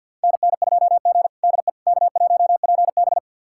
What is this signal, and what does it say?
Morse code, heard as a keyed tone, about 46 wpm, 700 Hz. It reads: NN2G DE K1PB
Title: USA Callsign DE Callsign - 4 Characters - 50wpm